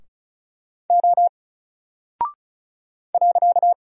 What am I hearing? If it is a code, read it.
Morse code, heard as a keyed tone, about 36 wpm, 700 Hz. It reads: O .
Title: Single Character (Letter:Number:Punctuation:Prosign) 35wpm